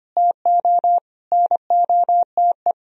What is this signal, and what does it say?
Morse code, heard as a keyed tone, about 26 wpm, 700 Hz. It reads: TO NOTE
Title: Speed-Racing - Sets of 5 Words 25wpm